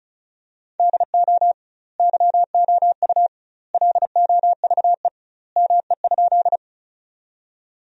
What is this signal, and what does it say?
Morse code, heard as a keyed tone, about 34 wpm, 700 Hz. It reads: DO YOU LOVE ME?
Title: 2x Word-Spacing - Sentences from Top 400 Words - 35wpm